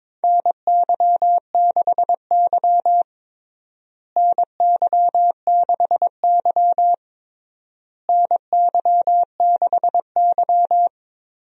Morse code, heard as a keyed tone, about 22 wpm, 700 Hz. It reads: NY6Y NY6Y NY6Y
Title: USA Call Signs - 4 Characters - Repeated 3x - 22wpm